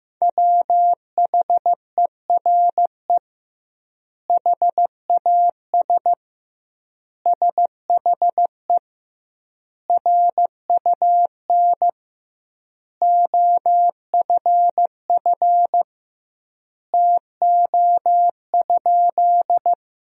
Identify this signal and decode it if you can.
Morse code, heard as a keyed tone, about 15 wpm, 700 Hz. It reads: WHERE HAS SHE RUN OFF TO?
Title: 2x Word-Spacing - Sentences from Top 300 Words - 15wpm